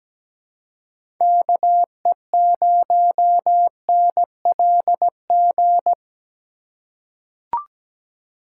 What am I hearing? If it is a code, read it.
Morse code, heard as a keyed tone, about 17 wpm, 700 Hz. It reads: KE0NLG
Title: USA Call Signs - 6 Characters - 3x Character Spacing - 17wpm